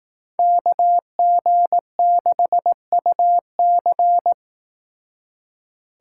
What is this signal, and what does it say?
Morse code, heard as a keyed tone, about 19 wpm, 700 Hz. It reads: KG6UC
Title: Speed Racing - USA 2x2 Call Signs 15wpm